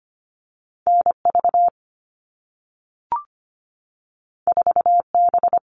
Morse code, heard as a keyed tone, about 25 wpm, 700 Hz. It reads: NV 4B
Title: All Cut Numbers - 2-Digit 25wpm